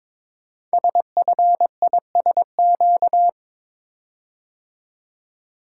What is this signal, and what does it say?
Morse code, heard as a keyed tone, about 22 wpm, 700 Hz. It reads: SFISQ
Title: Code Groups - 5 Letter:Numbers - 22wpm